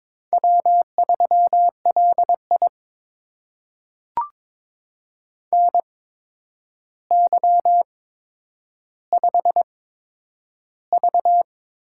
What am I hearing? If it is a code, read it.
Morse code, heard as a keyed tone, about 22 wpm, spaced as for 9 wpm, 700 Hz. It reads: W3LI NY5V
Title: USA Call Signs - 4 Characters - 8x Character Spacing - 22wpm